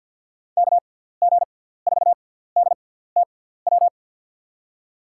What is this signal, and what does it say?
Morse code, heard as a keyed tone, about 52 wpm, spaced as for 17 wpm, 700 Hz. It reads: KG4DTW
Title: USA Call Signs - 6 Characters - 6x Character Spacing - 50wpm